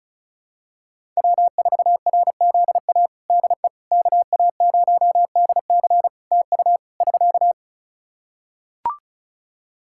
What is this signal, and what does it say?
Morse code, heard as a keyed tone, about 36 wpm, 700 Hz. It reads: W4RZA DE KA0DC TU <SK>
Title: Practice QSO - Exchange 4 - Ending - 35wpm